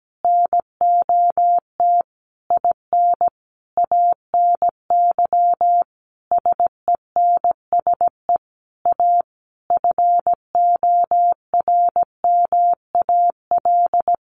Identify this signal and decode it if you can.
Morse code, heard as a keyed tone, about 17 wpm, 700 Hz. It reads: NOT IN ANY SENSE A FORMAL
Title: Sentences from Top 1300 Words 17wpm